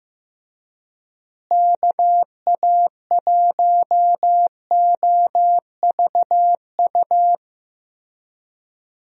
Morse code, heard as a keyed tone, about 15 wpm, 700 Hz. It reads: KA1OVU
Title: USA Call Signs - 6 Characters - Repeated 3x - 15wpm